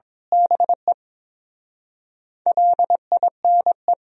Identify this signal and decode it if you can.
Morse code, heard as a keyed tone, about 22 wpm, 700 Hz. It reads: BE LINE
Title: Speed-Racing - Sets of 5 Words 22wpm